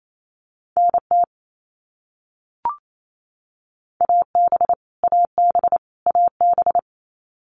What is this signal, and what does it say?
Morse code, heard as a keyed tone, about 28 wpm, 700 Hz. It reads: NT AB AB AB
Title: Canadian Provinces and Territories - Repeated 3x - 28wpm